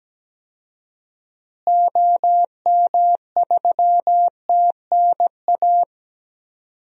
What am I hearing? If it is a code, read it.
Morse code, heard as a keyed tone, about 17 wpm, 700 Hz. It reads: OM3TNA
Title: DXCC-100 Call Signs - 6 Characters - Repeated 3x - 17wpm